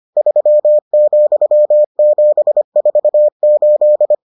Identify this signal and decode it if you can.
Morse code, heard as a keyed tone, about 25 wpm, 600 Hz. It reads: 3,748